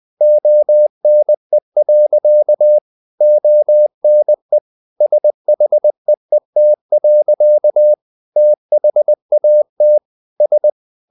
Morse code, heard as a keyed tone, about 20 wpm, 600 Hz. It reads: ONE. ONE SHEET. THAT S
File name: the old man and the sea / the old man and the sea part 1 20wpm